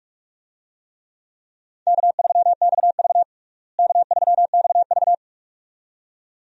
Morse code, heard as a keyed tone, about 42 wpm, 700 Hz. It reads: K3XV K3XV
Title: USA Call Signs - 4 Characters - Repeated 2x - 45wpm